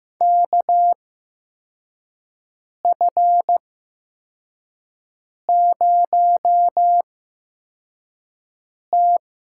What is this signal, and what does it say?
Morse code, heard as a keyed tone, about 15 wpm, spaced as for 4 wpm, 700 Hz. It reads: KF0T
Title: USA Call Signs - 5 Characters - 8x Character Spacing - 15wpm